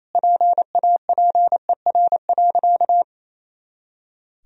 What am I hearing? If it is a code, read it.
Morse code, heard as a keyed tone, about 28 wpm, 700 Hz. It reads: PAPER.
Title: Sentences from Top 300 Words and Abbreviations 28wpm